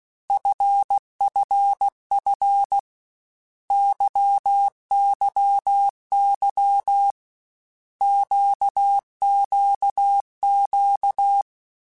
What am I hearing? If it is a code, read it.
Morse code, heard as a keyed tone, about 16 wpm, 800 Hz. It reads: FFF YYY QQQ
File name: Basic Letter Groups By7Groups / LFCYQ / LFCYQ@13